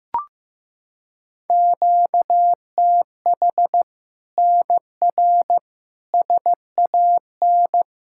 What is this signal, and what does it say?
Morse code, heard as a keyed tone, about 16 wpm, 700 Hz. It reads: QTH NR SAN
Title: Practice QSO - Exchange 1 - Beginning - 15wpm